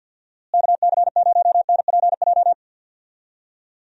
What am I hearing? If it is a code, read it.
Morse code, heard as a keyed tone, about 50 wpm, 700 Hz. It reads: KC0NPJ
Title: USA Call Signs - 6 Characters - Repeated 3x - 50wpm